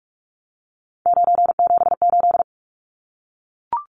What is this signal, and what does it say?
Morse code, heard as a keyed tone, about 44 wpm, 700 Hz. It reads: 978
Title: Numbers - 3-Digit 45wpm